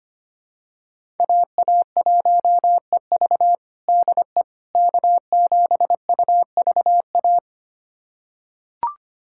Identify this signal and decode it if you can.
Morse code, heard as a keyed tone, about 25 wpm, 700 Hz. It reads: AA1EV DE K7UVA
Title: USA Callsign DE Callsign - 5 Characters - 25wpm